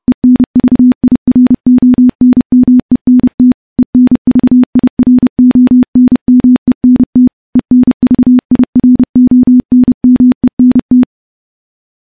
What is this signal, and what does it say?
Morse code, heard as a keyed tone, about 30 wpm, 250 Hz. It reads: ENVIRONMENT ENVIRONMENT ENVIRONMENT
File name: morse / morse-training-30WPM